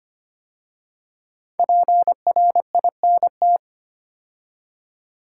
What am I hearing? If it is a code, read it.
Morse code, heard as a keyed tone, about 25 wpm, 700 Hz. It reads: PRINT
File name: 5 Letter Words - First 3 Letters Given - 25wpm